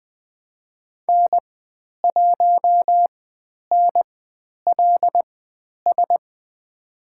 Morse code, heard as a keyed tone, about 20 wpm, spaced as for 10 wpm, 700 Hz. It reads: N1NLS
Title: 095 Lesson 16g - Letter c - Call Signs 20wpm